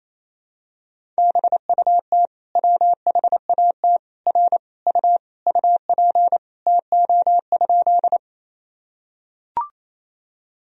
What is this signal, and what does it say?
Morse code, heard as a keyed tone, about 28 wpm, 700 Hz. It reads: BUT WHAT R U UP TO?
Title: Sentences from Top 100 Words and Abbreviations 28wpm